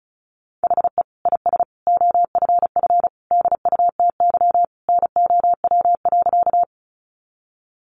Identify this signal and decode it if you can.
Morse code, heard as a keyed tone, about 34 wpm, 700 Hz. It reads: HE IS OFF DUTY NOW.
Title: Sentences from Top 1500 Words 35wpm